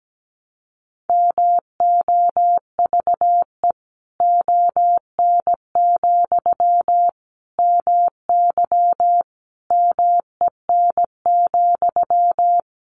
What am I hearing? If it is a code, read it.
Morse code, heard as a keyed tone, about 17 wpm, 700 Hz. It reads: MOVE ON, MY MEN,
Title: Sentences from Top 500 Words 17wpm